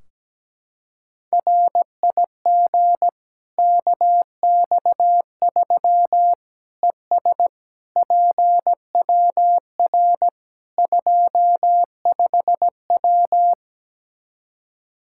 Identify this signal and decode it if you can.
Morse code, heard as a keyed tone, about 17 wpm, 700 Hz. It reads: RIG KX3 ES PWR 25W
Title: K1USN SST QSO Practice 17wpm